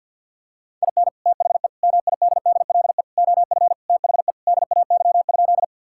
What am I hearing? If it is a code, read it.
Morse code, heard as a keyed tone, about 45 wpm, 700 Hz. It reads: IN THE MIDDLE OF THE DAY?
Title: Sentences from Top 1000 Words - Review Part 6 - 50wpm